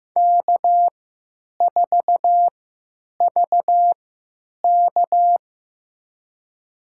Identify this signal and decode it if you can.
Morse code, heard as a keyed tone, about 15 wpm, spaced as for 9 wpm, 700 Hz. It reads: K4VK
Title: USA Call Signs - 4 Characters - 3x Character Spacing - 15wpm